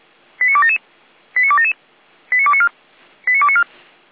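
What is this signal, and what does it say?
Five-tone selective call (ZVEI), heard as quick runs of five tones, tones of about 70 ms. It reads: 89277 89277 89294 89294
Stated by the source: ZVEI five-tone call received by SDR.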